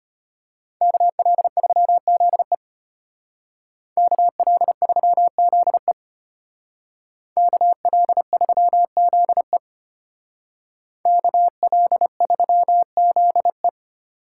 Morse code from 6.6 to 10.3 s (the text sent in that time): KL3ZE